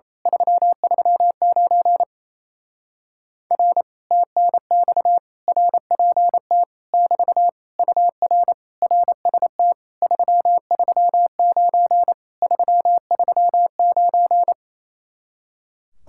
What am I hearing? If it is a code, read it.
Morse code, heard as a keyed tone, about 26 wpm, 700 Hz. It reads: 339 R TNX RPT <BT> UR RST 339 339
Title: Speed-Racing Common QSO Phrases Encore - 28wpm